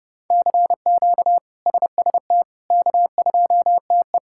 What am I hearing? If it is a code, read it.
Morse code, heard as a keyed tone, about 30 wpm, 700 Hz. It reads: CQ SST K2TE
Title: K1USN SST QSO Practice - 30wpm